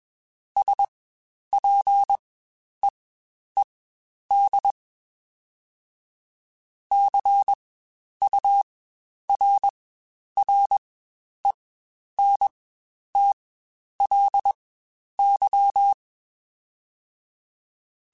Morse code, heard as a keyed tone, about 21 wpm, spaced as for 10 wpm, 800 Hz. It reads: SPEED CURRENTLY
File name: Spaced Characters / Spaced Characters-mp3s / Weather5 5 sp @20